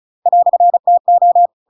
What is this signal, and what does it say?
Morse code, heard as a keyed tone, about 34 wpm, 700 Hz. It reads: "TO
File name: Pinocchio / Pinocchio ch8 35 35 ik1hge